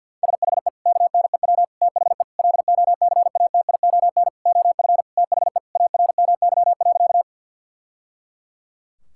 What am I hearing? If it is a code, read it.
Morse code, heard as a keyed tone, about 47 wpm, 700 Hz. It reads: SHE KNEW THE LOCATION OF THE ARMY.
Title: Sentences from Top 1000 Words - Review Part 6 - 50wpm